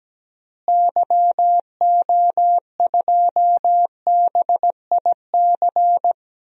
Morse code, heard as a keyed tone, about 17 wpm, 700 Hz. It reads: YO2BIC
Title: DXCC-100 Call Signs - 6 Characters - Repeated 3x - 17wpm